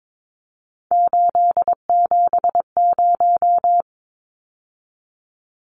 Morse code, heard as a keyed tone, about 22 wpm, 700 Hz. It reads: 870
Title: Numbers - 3-Digit 22wpm